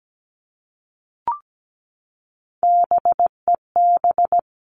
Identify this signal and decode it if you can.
Morse code, heard as a keyed tone, about 17 wpm, 700 Hz. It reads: BEB
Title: All Cut Numbers - 3-Digit 17wpm